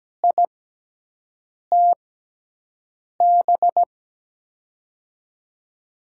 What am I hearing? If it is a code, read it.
Morse code, heard as a keyed tone, about 17 wpm, spaced as for 6 wpm, 700 Hz. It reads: ITB
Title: USA Call Signs - 6x Character Spacing - 17wpm